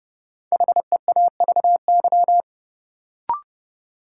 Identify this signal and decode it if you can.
Morse code, heard as a keyed tone, about 30 wpm, 700 Hz. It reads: HEAVY
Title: Sentences from Top 1100 Words - New Word Drill Alt - 30wpm